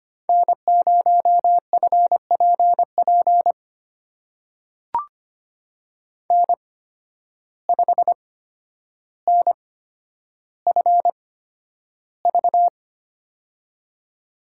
Morse code, heard as a keyed tone, about 25 wpm, spaced as for 11 wpm, 700 Hz. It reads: N0FPP N5NFV
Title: USA Call Signs - 5 Characters - 8x Character Spacing - 25wpm